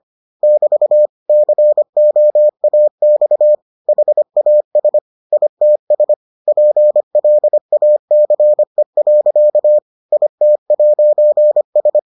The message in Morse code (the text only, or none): <BT> COAX HAS ITS PLACE. IT'S